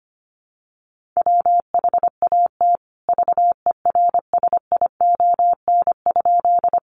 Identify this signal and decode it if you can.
Morse code, heard as a keyed tone, about 25 wpm, 700 Hz. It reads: WHAT VERSION?
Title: Sentences from Top 1400 Words 25wpm